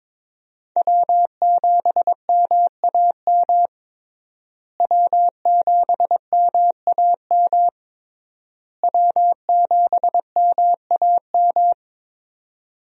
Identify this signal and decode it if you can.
Morse code, heard as a keyed tone, about 21 wpm, 700 Hz. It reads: W7MAM W7MAM W7MAM
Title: USA Call Signs - 5 Characters - Repeated 3x - 22wpm